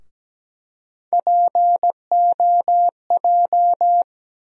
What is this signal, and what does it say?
Morse code, heard as a keyed tone, about 17 wpm, 700 Hz. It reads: POJ
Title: USA Call Sign Postfixes - 3 Letters 17wpm